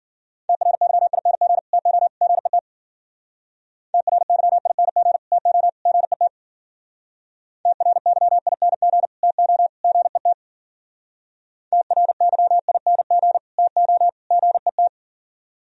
Morse code, heard as a keyed tone, about 50 wpm, 700 Hz. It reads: TRYING TO GET TRYING TO GET TRYING TO GET TRYING TO GET